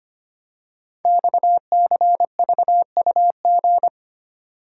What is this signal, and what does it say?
Morse code, heard as a keyed tone, about 25 wpm, 700 Hz. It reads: XCVUG
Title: Code Groups - 5 Letter:Numbers - 25wpm